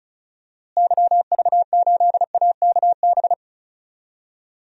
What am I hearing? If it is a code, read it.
Morse code, heard as a keyed tone, about 36 wpm, 700 Hz. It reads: YV8AKB
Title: DXCC-100 Call Signs - 6 Characters - Repeated 3x - 35wpm